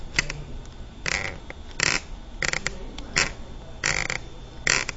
Emergency Alert System (SAME) tones, heard as harsh buzzing bursts. It is heard from 1.0 to 4.2 s.